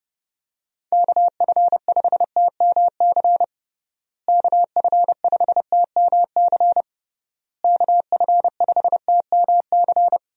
Morse code, heard as a keyed tone, about 32 wpm, 700 Hz. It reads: KF5TMC KF5TMC KF5TMC
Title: USA Call Signs - 6 Characters - Repeated 3x - 30wpm